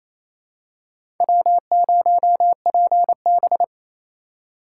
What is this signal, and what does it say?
Morse code, heard as a keyed tone, about 28 wpm, 700 Hz. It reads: W0PB
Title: USA Call Signs - 4 Characters - 2x Character Spacing - 28wpm